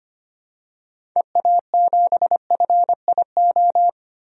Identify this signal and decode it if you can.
Morse code, heard as a keyed tone, about 25 wpm, 700 Hz. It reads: EA7FIO
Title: DXCC-100 Call Signs - 6 Characters - Repeated 2x - 25wpm